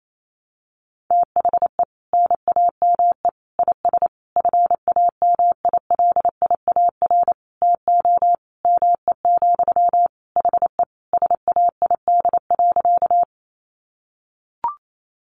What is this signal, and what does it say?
Morse code, heard as a keyed tone, about 28 wpm, 700 Hz. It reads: THE NAME IS FAMILIAR TO ME, HE SAID.
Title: Sentences from Top 1000 Words 28wpm